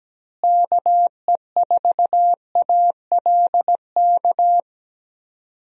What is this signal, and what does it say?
Morse code, heard as a keyed tone, about 17 wpm, 700 Hz. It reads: KE4ALK
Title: USA Call Signs - 6 Characters - Repeated 3x - 17wpm